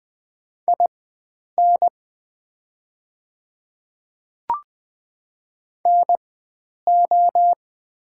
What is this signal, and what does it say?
Morse code, heard as a keyed tone, about 20 wpm, spaced as for 9 wpm, 690 Hz. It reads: IN NO
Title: 4x Character Spacing - Top 100 Words - 20wpm